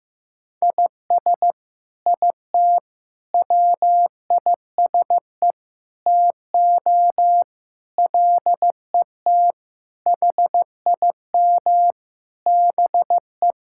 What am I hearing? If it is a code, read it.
Morse code, heard as a keyed tone, about 15 wpm, 700 Hz. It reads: IS IT WISE TO LET HIM BE
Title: Sentences from Top 1600 Words 15wpm